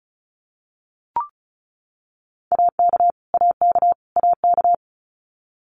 Morse code, heard as a keyed tone, about 34 wpm, 700 Hz. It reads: AK AK AK
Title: US States Repeated 3x 35wpm